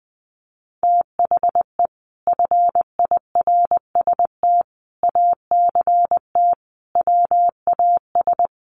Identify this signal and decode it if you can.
Morse code, heard as a keyed tone, about 20 wpm, 700 Hz. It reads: THE FIRST ACT WAS
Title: Sentences from Top 200 Words 20wpm